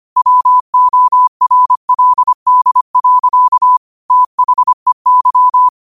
Morse code, heard as a keyed tone, about 25 wpm, 1000 Hz. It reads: WORLD. THEY
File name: qotdfiles / 25wpm